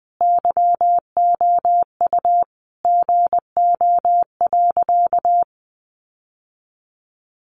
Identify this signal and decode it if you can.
Morse code, heard as a keyed tone, about 20 wpm, 700 Hz. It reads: YOU GO.
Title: Sentences from Top 600 Words 20wpm